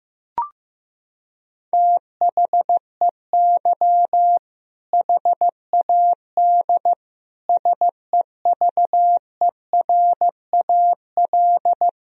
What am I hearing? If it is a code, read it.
Morse code, heard as a keyed tone, about 15 wpm, 700 Hz. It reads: THEY HAD SEVERAL
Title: Sentences from Top 1000 Words - Review Part 2 - 15wpm